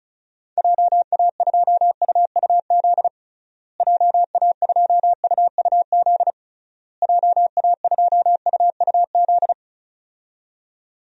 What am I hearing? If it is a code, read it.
Morse code, heard as a keyed tone, about 36 wpm, 700 Hz. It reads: JA2UUZ JA2UUZ JA2UUZ
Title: DXCC-100 Call Signs - 6 Characters - Repeated 3x - 35wpm